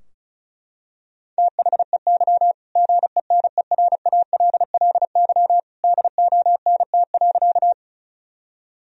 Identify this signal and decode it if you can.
Morse code, heard as a keyed tone, about 36 wpm, 700 Hz. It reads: THEY GENERALLY DONT.
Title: Sentences from Top 1000 Words - Review Part 4 - 35wpm